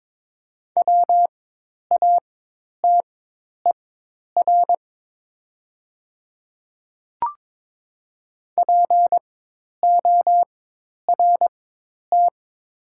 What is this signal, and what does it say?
Morse code, heard as a keyed tone, about 22 wpm, spaced as for 10 wpm, 700 Hz. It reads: WATER PORT
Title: 4x Character Spacing - Top 200 Words - 22wpm